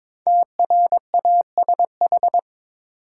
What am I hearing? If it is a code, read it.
Morse code, heard as a keyed tone, about 22 wpm, 700 Hz. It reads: TRASH
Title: Five Letter Words - 22wpm